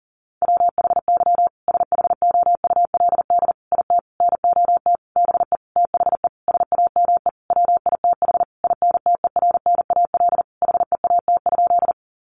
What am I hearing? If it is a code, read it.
Morse code, heard as a keyed tone, about 40 wpm, 700 Hz. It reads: WHY SHOULD IT NOT BE THE SAME WITH INTERNAL HEAT?
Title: Sentences from Top 1400 Words 40wpm